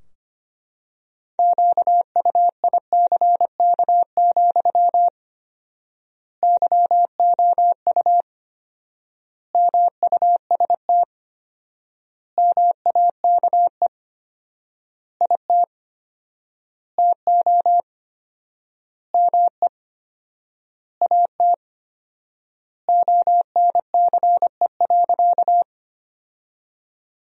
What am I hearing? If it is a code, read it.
Morse code, heard as a keyed tone, about 25 wpm, 700 Hz. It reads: QUICK, YOU MUST MAKE IT TO ME AT ONCE.
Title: 4x Word-Spacing - Sentences from Top 500 Words - 25wpm